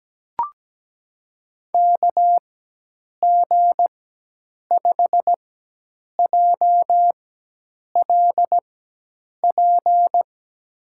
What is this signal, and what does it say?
Morse code, heard as a keyed tone, about 17 wpm, spaced as for 8 wpm, 700 Hz. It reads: KG5JLP
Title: USA Call Signs - 6 Characters - 4x Character Spacing - 17wpm